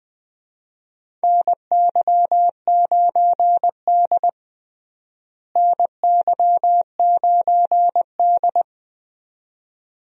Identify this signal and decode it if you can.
Morse code, heard as a keyed tone, about 20 wpm, 700 Hz. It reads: NY9D NY9D
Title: USA Call Signs - 4 Characters - Repeated 2x - 20wpm